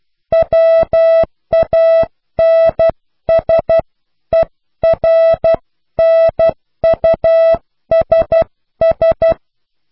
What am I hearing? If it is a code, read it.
Morse code, heard as a keyed tone, about 12 wpm, 660 Hz. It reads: WANSERNUSS
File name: cw / pseudo worden(10)